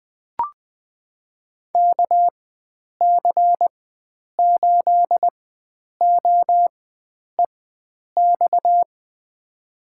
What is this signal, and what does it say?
Morse code, heard as a keyed tone, about 20 wpm, spaced as for 9 wpm, 700 Hz. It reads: KC8OEX
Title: USA Call Signs - 6 Characters - 4x Character Spacing - 20wpm